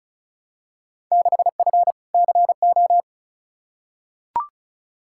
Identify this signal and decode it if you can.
Morse code, heard as a keyed tone, about 36 wpm, 690 Hz. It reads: 6F CO
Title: Field Day Exchanges Encore - 35wpm